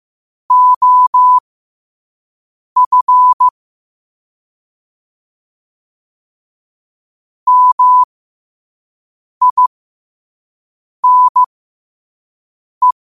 Morse code, heard as a keyed tone, about 15 wpm, spaced as for 5 wpm, 1000 Hz. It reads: OF MINE